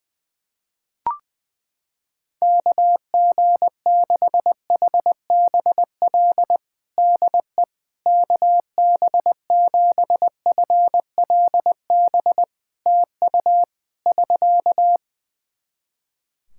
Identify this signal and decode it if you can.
Morse code, heard as a keyed tone, about 20 wpm, 700 Hz. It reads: KG6HBL DE KB7FLB TU <SK>
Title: Practice QSO - Exchange 4 - Ending - 20wpm